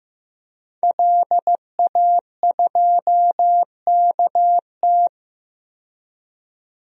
Morse code, heard as a keyed tone, about 15 wpm, 700 Hz. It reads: LA2KT
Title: DXCC-100 Call Signs - 5 Characters - 15wpm